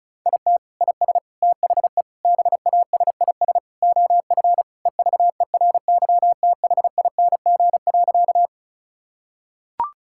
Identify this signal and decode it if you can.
Morse code, heard as a keyed tone, about 36 wpm, 700 Hz. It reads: IT IS THE BASIS OF EVERYTHING.
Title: Sentences from Top 1000 Words - Review Part 5 - 35wpm